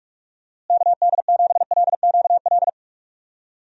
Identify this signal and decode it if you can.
Morse code, heard as a keyed tone, about 46 wpm, 700 Hz. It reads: KD7LQL
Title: USA Call Signs - 6 Characters - Repeated 2x - 45wpm